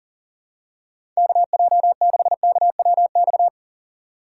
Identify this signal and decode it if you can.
Morse code, heard as a keyed tone, about 40 wpm, 700 Hz. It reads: KJ6KWX
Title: USA Call Signs - 6 Characters - 3x Character Spacing - 40wpm